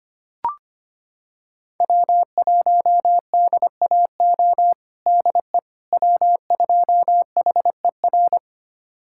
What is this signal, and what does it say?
Morse code, heard as a keyed tone, about 25 wpm, 700 Hz. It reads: W1DAO DE W2HER
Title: USA Callsign DE Callsign - 5 Characters - 25wpm